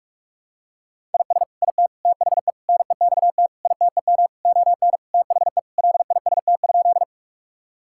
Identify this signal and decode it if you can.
Morse code, heard as a keyed tone, about 46 wpm, 700 Hz. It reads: IS IT THE NEXT ITEM ON THE LIST?
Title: Sentences from Top 1000 Words - Review Part 6 - 45wpm